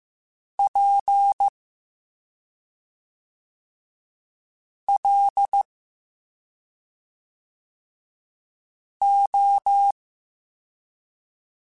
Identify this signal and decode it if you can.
Morse code, heard as a keyed tone, about 15 wpm, spaced as for 2 wpm, 800 Hz. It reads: PLO